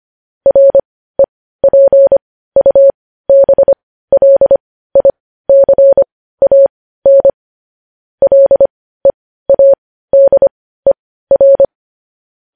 Morse code, heard as a keyed tone, about 25 wpm, spaced as for 15 wpm, 550 Hz. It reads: REPUBLICAN LEADER